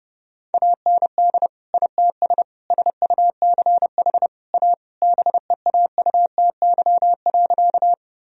Morse code, heard as a keyed tone, about 32 wpm, 700 Hz. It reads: AND ITS SUCH A BEAUTY.
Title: Sentences from Top 1000 Words - Review Part 2 - 30wpm